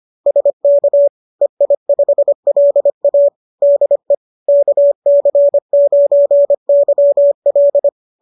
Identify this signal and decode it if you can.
Morse code, heard as a keyed tone, about 25 wpm, 570 Hz. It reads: SK EI5LA DE KC9YL